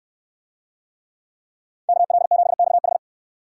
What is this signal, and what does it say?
Morse code, heard as a keyed tone, about 68 wpm, 700 Hz. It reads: KL7XH